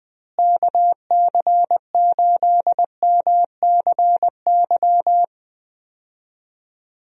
Morse code, heard as a keyed tone, about 20 wpm, 700 Hz. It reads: KC8MCY
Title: USA Call Signs - 6 Characters - Repeated 3x - 20wpm